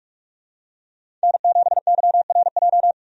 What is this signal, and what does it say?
Morse code, heard as a keyed tone, about 47 wpm, 700 Hz. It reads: N7YRJ
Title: USA Call Signs - 5 Characters - 45wpm